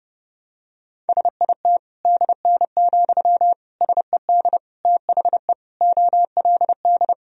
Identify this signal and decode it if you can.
Morse code, heard as a keyed tone, about 32 wpm, 700 Hz. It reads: SIT DN, SED THE OLD
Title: Sentences from Top 500 Words and Abbreviations 30wpm